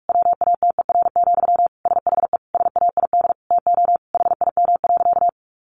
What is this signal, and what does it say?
Morse code, heard as a keyed tone, about 44 wpm, 700 Hz. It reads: WATER, SHE SAID TO HIM.
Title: Sentences from Top 800 Words 45wpm